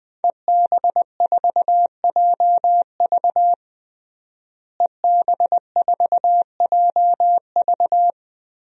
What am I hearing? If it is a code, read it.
Morse code, heard as a keyed tone, about 20 wpm, 700 Hz. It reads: EB4JV EB4JV
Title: DXCC-100 Call Signs - 5 Characters - Repeated 3x - 20wpm